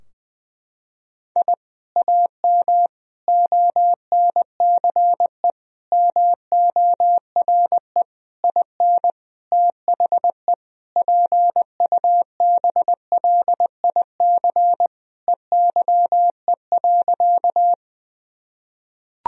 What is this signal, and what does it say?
Morse code, heard as a keyed tone, about 21 wpm, 700 Hz. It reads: I AM ONCE MORE IN THE PUBLIC EYE.
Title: Sentences from Top 1000 Words - Review Part 3 - 20wpm